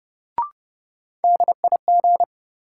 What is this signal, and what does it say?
Morse code, heard as a keyed tone, about 30 wpm, 700 Hz. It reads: DIG
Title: Sentences from Top 1600 Words - New Word Drill - 30wpm